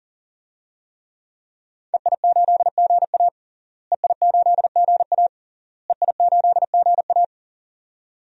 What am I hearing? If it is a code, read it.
Morse code, heard as a keyed tone, about 40 wpm, 700 Hz. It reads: EI8GA EI8GA EI8GA
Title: DXCC-100 Call Signs - 5 Characters - Repeated 3x - 40wpm